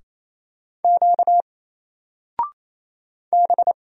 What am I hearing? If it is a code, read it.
Morse code, heard as a keyed tone, about 27 wpm, 700 Hz. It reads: Q B
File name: Single-Letter-Number-28wpm